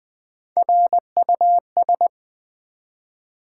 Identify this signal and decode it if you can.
Morse code, heard as a keyed tone, about 20 wpm, 700 Hz. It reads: RUS
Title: Build A Word - 1 Letter At A Time - 4-Letter Words - 20wpm